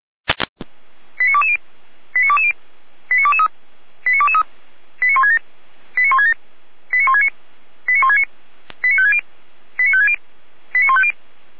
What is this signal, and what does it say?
Five-tone selective call (ZVEI), heard as quick runs of five tones, tones of about 70 ms. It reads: 89220 89220 89223 89223 89167 89167 89169 89169 89560 89560 89150
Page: ZVEI